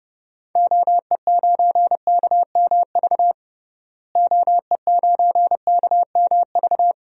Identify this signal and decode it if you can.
Morse code, heard as a keyed tone, about 30 wpm, 700 Hz. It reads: OE9KMV OE9KMV
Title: DXCC-100 Call Signs - 6 Characters - Repeated 3x - 30wpm